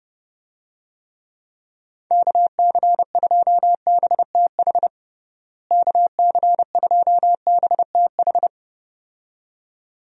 Morse code, heard as a keyed tone, about 31 wpm, 700 Hz. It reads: KC2BTH KC2BTH
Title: USA Call Signs - 6 Characters - Repeated 2x - 30wpm